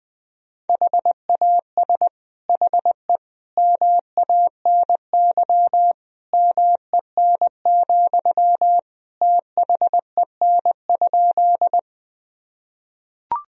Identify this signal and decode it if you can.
Morse code, heard as a keyed tone, about 20 wpm, 700 Hz. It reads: HAS HE MANY MEN, THEN?
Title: Sentences from Top 1000 Words - Review Part 1 - 20wpm